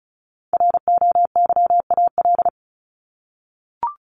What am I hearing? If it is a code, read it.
Morse code, heard as a keyed tone, about 34 wpm, 700 Hz. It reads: ROYAL
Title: Top 1751-2000 Words 35wpm